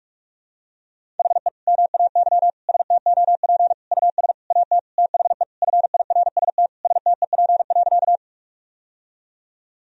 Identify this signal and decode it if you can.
Morse code, heard as a keyed tone, about 48 wpm, 700 Hz. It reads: HE MAY STOP US AT THE FIRST STEP.
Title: Sentences from Top 1000 Words - Review Part 2 - 45wpm